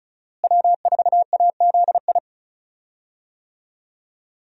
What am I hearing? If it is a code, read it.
Morse code, heard as a keyed tone, about 36 wpm, 700 Hz. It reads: W4AZI
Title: USA Call Signs - 5 Characters - Repeated 3x - 35wpm